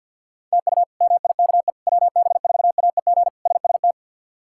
Morse code, heard as a keyed tone, about 47 wpm, 700 Hz. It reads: TU MIKE WB4AEG SST
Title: K1USN SST QSO Practice - 50wpm